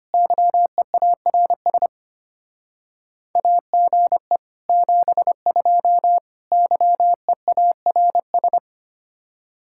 Morse code, heard as a keyed tone, about 32 wpm, 700 Hz. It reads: YEARS AGE 72 YEARS